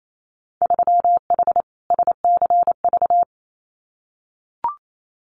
Morse code, heard as a keyed tone, about 28 wpm, 700 Hz. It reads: 3H SCV
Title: Winter Field Day Exchanges Encore 28wpm